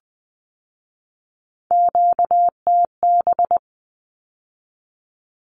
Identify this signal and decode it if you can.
Morse code, heard as a keyed tone, about 19 wpm, 700 Hz. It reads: QTB
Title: Q-Codes with Spoken Meaning - 20wpm